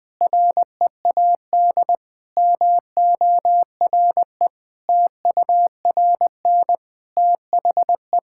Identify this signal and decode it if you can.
Morse code, heard as a keyed tone, about 21 wpm, 700 Hz. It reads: READ MORE TURN THE
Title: Sentences with 6-Word Phrases - 20wpm